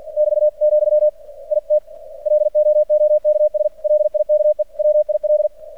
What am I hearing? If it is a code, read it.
Morse code, heard as a keyed tone, about 45 wpm, 600 Hz. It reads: <BT> , IT LOOKS LIKE WIL
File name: archived / CRSnet